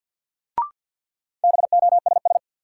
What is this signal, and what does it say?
Morse code, heard as a keyed tone, about 51 wpm, 700 Hz. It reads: BOSS
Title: Sentences from Top 700 Words - New Word Drill - 50wpm